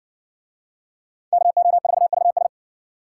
Morse code, heard as a keyed tone, about 55 wpm, 700 Hz. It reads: KO4VS